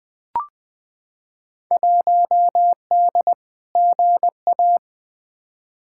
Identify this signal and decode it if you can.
Morse code, heard as a keyed tone, about 21 wpm, 700 Hz. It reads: 1D GA
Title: Field Day Exchanges - 20wpm